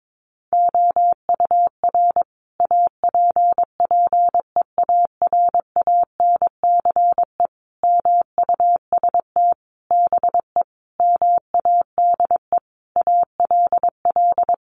Morse code, heard as a keyed tone, about 22 wpm, 700 Hz. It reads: OUR APPEARANCE MUST BE MADE ALL
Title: Sentences from Top 1100 Words 22wpm